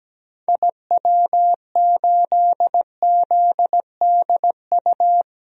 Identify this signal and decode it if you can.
Morse code, heard as a keyed tone, about 17 wpm, 700 Hz. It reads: IW8ZDU
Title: DXCC-100 Call Signs - 6 Characters - 17wpm